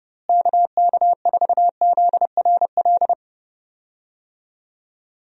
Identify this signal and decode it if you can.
Morse code, heard as a keyed tone, about 32 wpm, 700 Hz. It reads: KK4ZRL
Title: USA Call Signs - 6 Characters - Repeated 2x - 30wpm